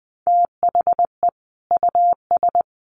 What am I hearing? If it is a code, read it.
Morse code, heard as a keyed tone, about 20 wpm, 700 Hz. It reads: THE US
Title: Sets of 5 Words Encore2 20wpm